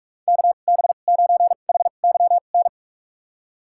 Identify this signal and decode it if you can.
Morse code, heard as a keyed tone, about 46 wpm, spaced as for 32 wpm, 700 Hz. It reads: KB9HYN
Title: USA Call Signs - 2x Character Spacing - 45wpm